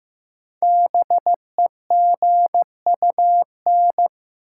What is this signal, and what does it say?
Morse code, heard as a keyed tone, about 15 wpm, 700 Hz. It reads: BEGUN
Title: Sentences from Top 1000 Words - Review Part 6 - New Word Drill - 15wpm